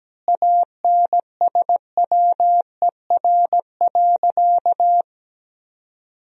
Morse code, heard as a keyed tone, about 17 wpm, 700 Hz. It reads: ANSWER.
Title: 2x Word-Spacing - Sentences from Top 500 Words - 17wpm